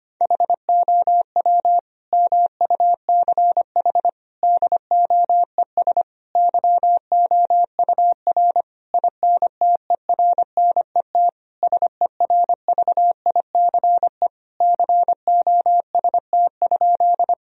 Morse code, heard as a keyed tone, about 25 wpm, 700 Hz. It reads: HOW MUCH DOES YOUR INTERNET SERVICE COST?
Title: Sentences from Top 1000 Words - Review Part 4 - 25wpm